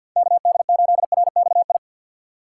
Morse code, heard as a keyed tone, about 65 wpm, 700 Hz. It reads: KD8RXI